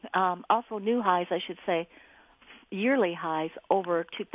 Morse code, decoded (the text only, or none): none